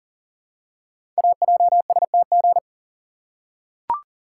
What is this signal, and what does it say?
Morse code, heard as a keyed tone, about 42 wpm, 700 Hz. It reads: AJSTG
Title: Code Groups - 5 Letters - 40wpm